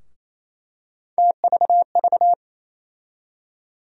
Morse code, heard as a keyed tone, about 28 wpm, 700 Hz. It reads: TVV
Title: USA Call Sign Postfixes - 3 Letters 28wpm